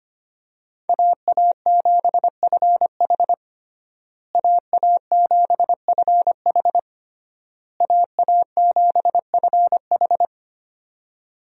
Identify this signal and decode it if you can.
Morse code, heard as a keyed tone, about 25 wpm, 700 Hz. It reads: AA7FH AA7FH AA7FH
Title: USA Call Signs - 5 Characters - Repeated 3x - 25wpm